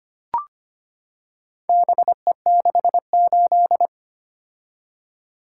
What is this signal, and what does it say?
Morse code, heard as a keyed tone, about 25 wpm, 700 Hz. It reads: BE68
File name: practice / 25wpm / Four Letter-Number Combinations - 25wpm